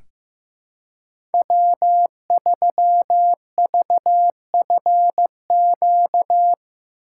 Morse code, heard as a keyed tone, about 15 wpm, 700 Hz. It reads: W3VFQ
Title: USA Call Signs - 5 Characters - 3x Character Spacing - 15wpm